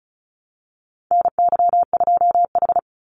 Morse code, heard as a keyed tone, about 34 wpm, 700 Hz. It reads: NY2H
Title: Call Signs Repeated 2x - 35wpm